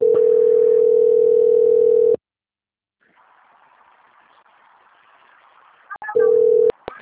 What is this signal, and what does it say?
Ringback tone (North America), 440 + 480 Hz.